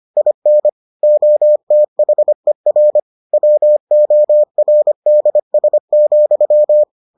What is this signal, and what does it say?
Morse code, heard as a keyed tone, about 25 wpm, 600 Hz. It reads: IN OTHER WORDS,